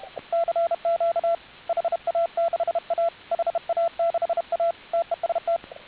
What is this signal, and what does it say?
Morse code, heard as a keyed tone, about 32 wpm, 690 Hz, with radio noise. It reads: CQ HA6A HA6A TEST